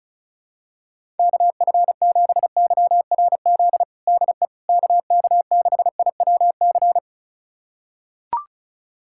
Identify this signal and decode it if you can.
Morse code, heard as a keyed tone, about 36 wpm, 700 Hz. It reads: KF7YRZ DE KK6IWC
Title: USA Callsign DE Callsign - 6 Characters - 35wpm